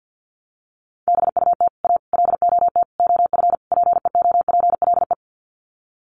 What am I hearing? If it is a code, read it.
Morse code, heard as a keyed tone, about 50 wpm, 700 Hz. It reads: BUT A LOT OF PEOPLE
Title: Sets of 5 Words Encore2 50wpm